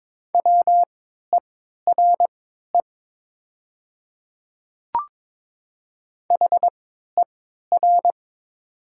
Morse code, heard as a keyed tone, about 22 wpm, spaced as for 13 wpm, 700 Hz. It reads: WERE HER
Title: 3x Character Spacing - Top 100 Words - 22wpm